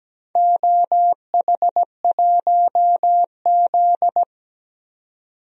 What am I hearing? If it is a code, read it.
Morse code, heard as a keyed tone, about 17 wpm, 700 Hz. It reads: OH1Z
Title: DXCC-100 Call Signs - 4 Characters - 17wpm